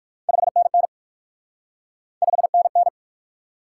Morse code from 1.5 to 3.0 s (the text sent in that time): 5NN